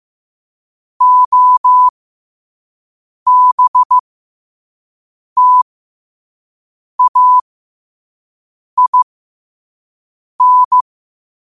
Morse code, heard as a keyed tone, about 15 wpm, spaced as for 5 wpm, 1000 Hz. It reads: OBTAIN
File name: qotdfiles / 5wpm